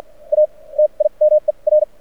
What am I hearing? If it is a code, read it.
Morse code, heard as a keyed tone, about 54 wpm, spaced as for 38 wpm, 610 Hz. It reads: A TIMER
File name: archived / CRSnet